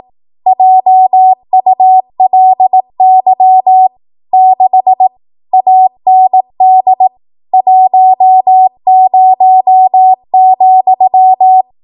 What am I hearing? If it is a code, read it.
Morse code, heard as a keyed tone, about 18 wpm, 750 Hz. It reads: JULY 6 AND 10,
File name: Morse / 18WPM